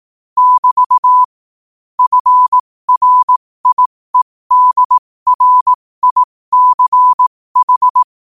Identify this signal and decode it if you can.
Morse code, heard as a keyed tone, about 18 wpm, spaced as for 15 wpm, 1000 Hz. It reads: <BT> FRIEDRICH
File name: qotdfiles / 15wpm